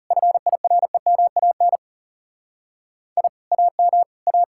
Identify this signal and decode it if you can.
Morse code, heard as a keyed tone, about 38 wpm, 700 Hz. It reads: FIREMAN I AM A